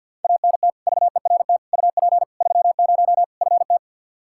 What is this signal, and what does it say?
Morse code, heard as a keyed tone, about 52 wpm, 700 Hz. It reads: ANT VERT UP 30 FT
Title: K1USN SST QSO Practice 50wpm